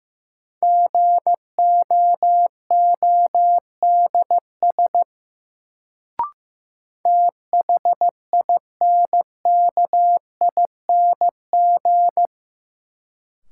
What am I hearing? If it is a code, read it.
Morse code, heard as a keyed tone, about 15 wpm, 700 Hz. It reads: GOODS THINKING
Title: Sentences from Top 1000 Words - Review Part 6 - New Word Drill - 15wpm